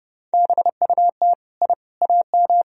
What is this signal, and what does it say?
Morse code, heard as a keyed tone, about 30 wpm, 700 Hz. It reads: BUT I AM
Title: Sentences from Top 1000 Words - Review Part 2 - 30wpm